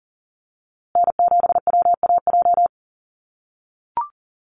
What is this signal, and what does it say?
Morse code, heard as a keyed tone, about 40 wpm, 700 Hz. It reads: N7WAJ
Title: USA Call Signs Repeated 2x 40wpm